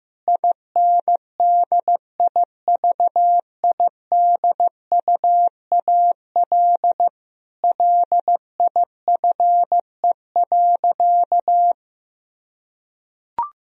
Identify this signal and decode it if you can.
Morse code, heard as a keyed tone, about 15 wpm, 700 Hz. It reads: INDIVIDUAL LIFE.
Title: Sentences from Top 1000 Words - Review Part 4 - 15wpm